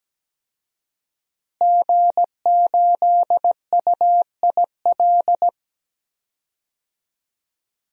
Morse code, heard as a keyed tone, about 17 wpm, 700 Hz. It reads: G8UIL
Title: DXCC-100 Call Signs - 5 Characters - Repeated 3x - 17wpm